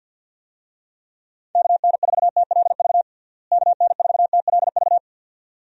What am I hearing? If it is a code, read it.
Morse code, heard as a keyed tone, about 50 wpm, 700 Hz. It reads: KN4TLV KN4TLV
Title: USA Call Signs - 6 Characters - Repeated 2x - 50wpm